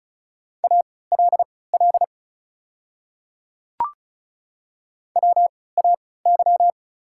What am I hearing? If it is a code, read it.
Morse code, heard as a keyed tone, about 35 wpm, spaced as for 20 wpm, 690 Hz. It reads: ALL WAY
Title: 3x Character Spacing - Top 100 Words - 35wpm